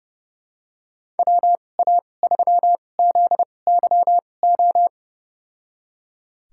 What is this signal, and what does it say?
Morse code, heard as a keyed tone, about 32 wpm, spaced as for 22 wpm, 700 Hz. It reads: WA3ZYO
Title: USA Call Signs - 6 Characters - 2x Character Spacing - 30wpm